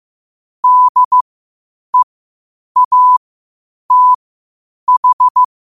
Morse code, heard as a keyed tone, about 15 wpm, spaced as for 8 wpm, 1000 Hz. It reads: DEATH
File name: qotdfiles / 8wpm